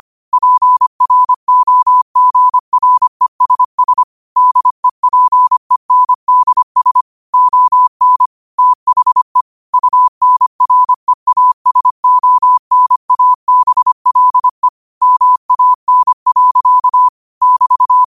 Morse code, heard as a keyed tone, about 25 wpm, 1000 Hz. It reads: PROGRESS DEPENDS ON THE UNREASONABLE MAN. <BT>
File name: qotdfiles / 25wpm